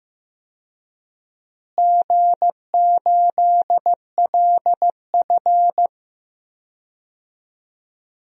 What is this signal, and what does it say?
Morse code, heard as a keyed tone, about 15 wpm, 700 Hz. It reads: G8LF
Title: DXCC-100 Call Signs - 4 Characters - Repeated 3x - 15wpm